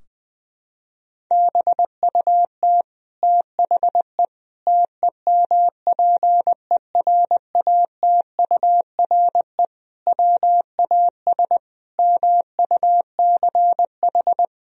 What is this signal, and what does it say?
Morse code, heard as a keyed tone, about 20 wpm, 700 Hz. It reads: BUT THE TEMPERATURE WAS MUCH
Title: Sentences from Top 1000 Words - Review Part 4 - 20wpm